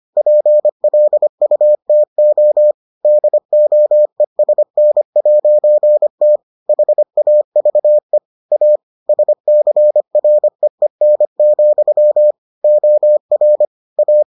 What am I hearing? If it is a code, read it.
Morse code, heard as a keyed tone, about 25 wpm, 600 Hz. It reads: PLUTO DOESN'T HAVE A SCREEN, OR A